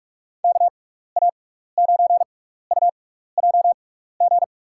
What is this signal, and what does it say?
Morse code, heard as a keyed tone, about 41 wpm, spaced as for 15 wpm, 700 Hz. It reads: KA9UJG
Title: USA Call Signs - 6 Characters - 6x Character Spacing - 45wpm